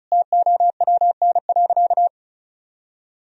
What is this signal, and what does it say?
Morse code, heard as a keyed tone, about 36 wpm, 700 Hz. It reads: TOWN.
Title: Sentences from Top 1000 Words - Review Part 4 - 35wpm